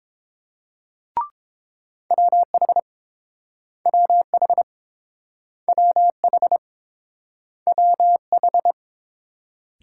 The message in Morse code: WH WH WH WH

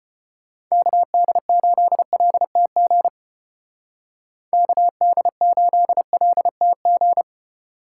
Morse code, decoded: KD8LTG KD8LTG

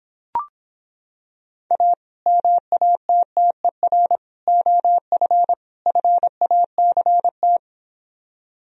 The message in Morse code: A MATTER OF FACT

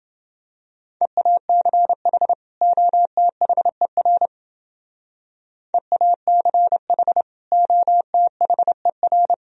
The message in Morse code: EACH OTHER EACH OTHER

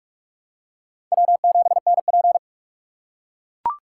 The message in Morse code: W7NP

